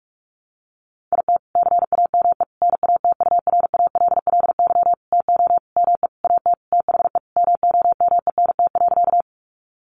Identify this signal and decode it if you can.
Morse code, heard as a keyed tone, about 44 wpm, 700 Hz. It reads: IT CAME NATURALLY TO ME AT THE MOMENT.
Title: Sentences from Top 1300 Words 45wpm